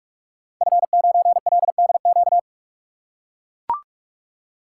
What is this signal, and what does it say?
Morse code, heard as a keyed tone, about 44 wpm, 700 Hz. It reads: F9LDQ
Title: Code Groups - 5 Letter:Numbers - 45wpm